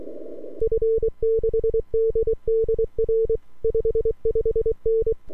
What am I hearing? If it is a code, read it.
Morse code, heard as a keyed tone, about 23 wpm, 450 Hz. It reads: F6DDR 55N
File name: n6tr / f6ddr